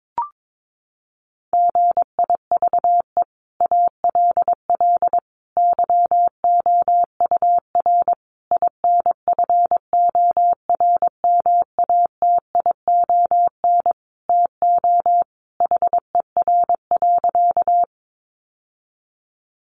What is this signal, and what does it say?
Morse code, heard as a keyed tone, about 22 wpm, 700 Hz. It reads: GIVE ALL YOUR INFORMATION TO HER.
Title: Sentences from Top 600 Words 22wpm